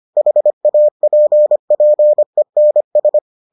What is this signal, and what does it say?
Morse code, heard as a keyed tone, about 25 wpm, 600 Hz. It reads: HAPPENS